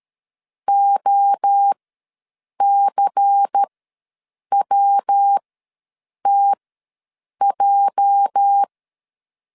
Morse code, heard as a keyed tone, about 13 wpm, spaced as for 7 wpm, 800 Hz. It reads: OCWTJ
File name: LEZIONI / 9 28 ws 12wpm 3354